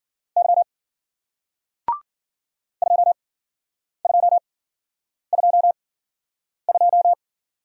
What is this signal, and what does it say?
Morse code, heard as a keyed tone, about 54 wpm, spaced as for 9 wpm, 700 Hz. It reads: Y 2222